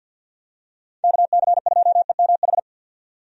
Morse code, heard as a keyed tone, about 53 wpm, 700 Hz. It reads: KC2EMH
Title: USA Call Signs - 6 Characters - 50wpm